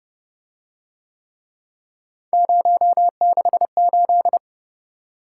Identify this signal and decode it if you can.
Morse code, heard as a keyed tone, about 27 wpm, 700 Hz. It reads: 068